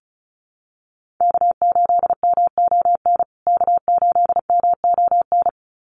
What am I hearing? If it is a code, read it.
Morse code, heard as a keyed tone, about 34 wpm, 700 Hz. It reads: K8MON K8MON
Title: USA Call Signs Repeated 2x 35wpm